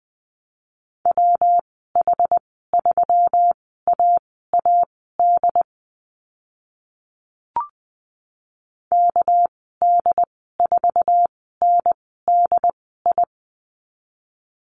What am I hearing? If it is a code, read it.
Morse code, heard as a keyed tone, about 20 wpm, spaced as for 14 wpm, 700 Hz. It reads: WH3AAD KD4NDI
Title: 219 Lesson 35g - Number 0 - Call Signs 20wpm